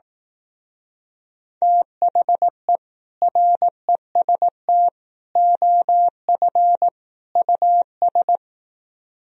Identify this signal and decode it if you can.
Morse code, heard as a keyed tone, about 18 wpm, 700 Hz. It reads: THE REST OF US
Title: Speed-Racing - Sets of 4 Words 15wpm